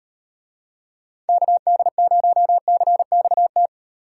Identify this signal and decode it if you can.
Morse code, heard as a keyed tone, about 36 wpm, 700 Hz. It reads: KD0CXT